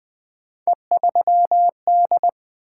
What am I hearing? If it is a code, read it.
Morse code, heard as a keyed tone, about 20 wpm, 700 Hz. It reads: E3D
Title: Three Letter-Number Combinations - 20wpm